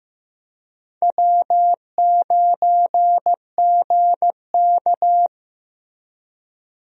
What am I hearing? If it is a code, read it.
Morse code, heard as a keyed tone, about 15 wpm, 700 Hz. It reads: W9GK
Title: USA Call Signs - 4 Characters - Repeated 3x - 15wpm